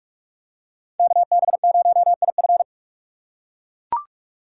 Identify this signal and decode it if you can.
Morse code, heard as a keyed tone, about 40 wpm, 700 Hz. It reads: KB0IF